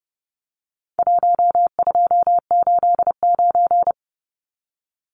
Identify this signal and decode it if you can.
Morse code, heard as a keyed tone, about 30 wpm, 700 Hz. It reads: 1289
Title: Numbers - Contest Style - 4-Digit 30wpm